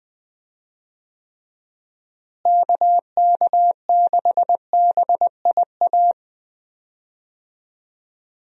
Morse code, heard as a keyed tone, about 20 wpm, 700 Hz. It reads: KK6BIA
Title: USA Call Signs - 6 Characters - Repeated 3x - 20wpm